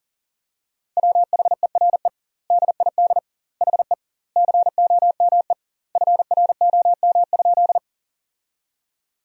Morse code, heard as a keyed tone, about 37 wpm, 700 Hz. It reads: WHERE DID HE COME FROM?
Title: 2x Word-Spacing - Sentences from Top 200 Words - 40wpm